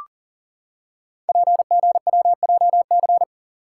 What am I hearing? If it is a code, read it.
Morse code, heard as a keyed tone, about 43 wpm, 700 Hz. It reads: PGWJC
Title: Code Groups - 5 Letters - 40wpm